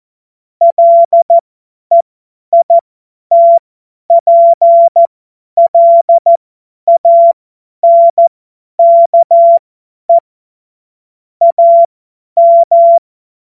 Morse code, heard as a keyed tone, about 14 wpm, spaced as for 10 wpm, 680 Hz. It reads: LEITPLANKE AM
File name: Morsekurs / Witze / Witz 24.Oktober 2023 50 F70